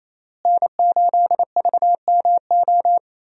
Morse code, heard as a keyed tone, about 28 wpm, 700 Hz. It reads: N8VMO
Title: USA Call Signs - 5 Characters - Repeated 3x - 28wpm